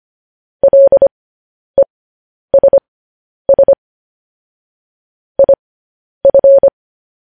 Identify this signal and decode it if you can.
Morse code, heard as a keyed tone, about 25 wpm, spaced as for 10 wpm, 550 Hz. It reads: LESS IF
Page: news-headlines-in-morse-code-at-10-wpm